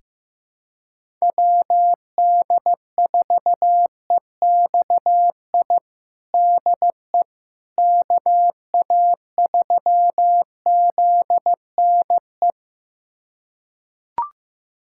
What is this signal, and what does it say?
Morse code, heard as a keyed tone, about 15 wpm, 700 Hz. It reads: WD4EXI DE KA3ZNE
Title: USA Callsign DE Callsign - 6 Characters - 15wpm